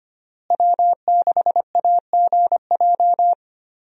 Morse code, heard as a keyed tone, about 25 wpm, 700 Hz. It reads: W6AGJ
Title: USA Call Signs - 5 Characters - Repeated 3x - 25wpm